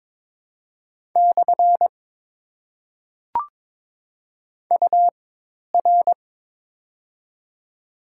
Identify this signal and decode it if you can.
Morse code, heard as a keyed tone, about 22 wpm, spaced as for 10 wpm, 690 Hz. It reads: / UR
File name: Top 90 QSO Elements - 4x Character Spacing - 22wpm